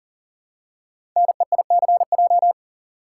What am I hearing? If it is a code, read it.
Morse code, heard as a keyed tone, about 42 wpm, 690 Hz. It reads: NEICJ
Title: Code Groups - 5 Letters - 40wpm